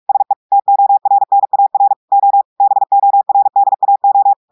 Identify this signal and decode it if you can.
Morse code, heard as a keyed tone, about 46 wpm, 800 Hz. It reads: SE TORNAR O BORDAO
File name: audios / portugais / portugais texte en clair 2 vitesse 45 mots minutes